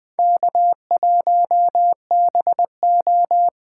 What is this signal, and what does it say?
Morse code, heard as a keyed tone, about 20 wpm, 700 Hz. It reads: K1BO